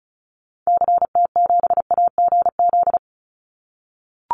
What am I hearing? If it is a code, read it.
Morse code, heard as a keyed tone, about 34 wpm, 700 Hz. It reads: CT7AGZ
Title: Call Signs - 35wpm